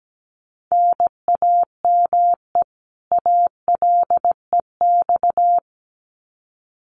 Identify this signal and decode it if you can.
Morse code, heard as a keyed tone, about 17 wpm, 700 Hz. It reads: NAME ALEX
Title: Common QSO Phrases 17wpm